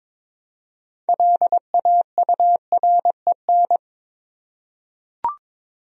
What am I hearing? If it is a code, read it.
Morse code, heard as a keyed tone, about 22 wpm, 700 Hz. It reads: LAUREN
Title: USA Common Names - 22wpm